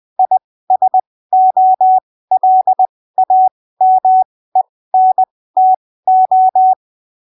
Morse code, heard as a keyed tone, about 20 wpm, spaced as for 15 wpm, 750 Hz. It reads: ISOLAMENTO
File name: arquivos / texto / texto 23 20ppm